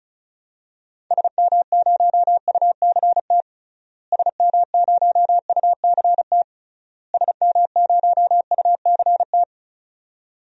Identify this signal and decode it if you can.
Morse code, heard as a keyed tone, about 36 wpm, 700 Hz. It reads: SM0UCT SM0UCT SM0UCT
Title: DXCC-100 Call Signs - 6 Characters - Repeated 3x - 35wpm